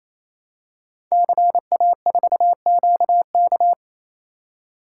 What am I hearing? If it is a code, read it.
Morse code, heard as a keyed tone, about 28 wpm, 700 Hz. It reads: CA4QK
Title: Code Groups - 5 Letter:Numbers - 28wpm